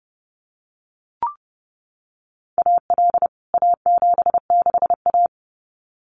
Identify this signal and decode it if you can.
Morse code, heard as a keyed tone, about 30 wpm, 700 Hz. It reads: AL A76A
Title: CWT Exchange 30wpm